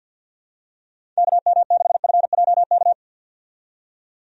Morse code, heard as a keyed tone, about 50 wpm, 700 Hz. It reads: KM6FJK
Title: USA Call Signs - 6 Characters - 50wpm